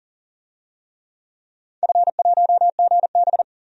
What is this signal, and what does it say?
Morse code, heard as a keyed tone, about 40 wpm, 700 Hz. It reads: F1GB
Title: DXCC-100 Call Signs - 4 Characters - 40wpm